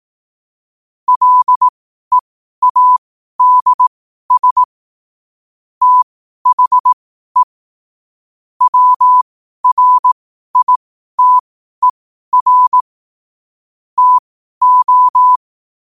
Morse code, heard as a keyed tone, about 18 wpm, spaced as for 12 wpm, 1000 Hz. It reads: LEADS THE WRITER TO
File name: qotdfiles / 12wpm